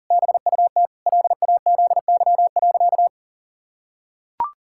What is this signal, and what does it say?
Morse code, heard as a keyed tone, about 40 wpm, 700 Hz. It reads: BUT LAZY.
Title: Pangram Challenge v2 - 40wpm